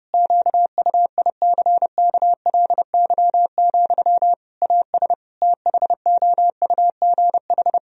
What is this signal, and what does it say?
Morse code, heard as a keyed tone, about 32 wpm, 700 Hz. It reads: QUICKLY, AS THOUGH
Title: Sentences from Top 1000 Words - Review Part 2 - 30wpm